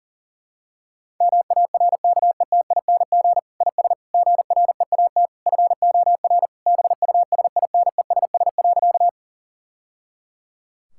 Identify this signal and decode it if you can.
Morse code, heard as a keyed tone, about 37 wpm, 700 Hz. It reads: MARKETING IS GREAT FOR BUSINESS.
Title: Sentences from Top 1000 Words - Review Part 5 - 40wpm